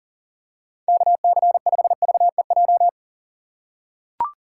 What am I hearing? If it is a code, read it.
Morse code, heard as a keyed tone, about 40 wpm, 700 Hz. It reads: KC5VEJ
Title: USA Call Signs - 6 Characters - 3x Character Spacing - 40wpm